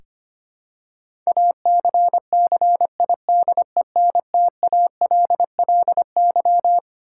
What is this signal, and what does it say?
Morse code, heard as a keyed tone, about 25 wpm, 700 Hz. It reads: ACCIDENTALLY
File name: Twelve Letter Words - 25wpm